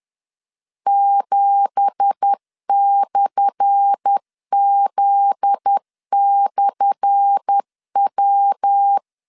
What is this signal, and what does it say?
Morse code, heard as a keyed tone, about 10 wpm, 800 Hz. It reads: 7/Z/W
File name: LEZIONI / 10 SIGNS / 10A SLASH 10wpm 2808